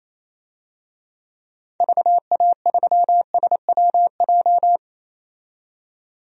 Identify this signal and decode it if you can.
Morse code, heard as a keyed tone, about 28 wpm, 700 Hz. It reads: VA3SWJ
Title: DXCC-100 Call Signs - 6 Characters - 28wpm